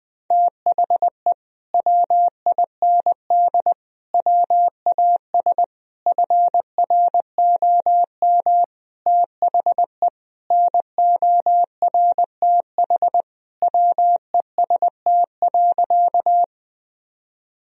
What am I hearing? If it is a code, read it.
Morse code, heard as a keyed tone, about 20 wpm, 700 Hz. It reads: THE WIND WAS FROM THE NORTH WEST.
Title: Sentences from Top 1000 Words - Review Part 2 - 20wpm